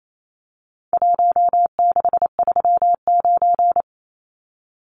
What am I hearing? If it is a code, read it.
Morse code, heard as a keyed tone, about 28 wpm, 700 Hz. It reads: 1639
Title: Numbers - Contest Style - 4-Digit 28wpm